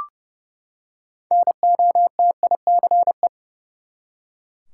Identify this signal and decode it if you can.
Morse code, heard as a keyed tone, about 30 wpm, 700 Hz. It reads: NOTICE
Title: Six Letter Words - 30wpm